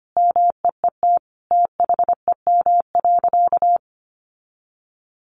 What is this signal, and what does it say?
Morse code, heard as a keyed tone, about 25 wpm, 700 Hz. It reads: MEET THEM.
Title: Sentences from Top 1100 Words 25wpm